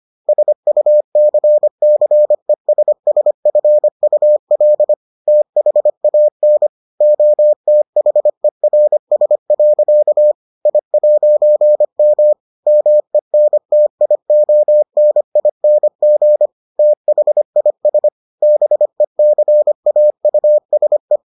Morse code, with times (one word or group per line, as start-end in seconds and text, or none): 0.3-4.9 SUCCESSFUL
5.3-6.7 THAN
7.0-10.3 OTHERS.
10.7-12.3 I'M
12.7-16.5 MENTIONING
16.8-18.1 THIS
18.4-21.2 BECAUSE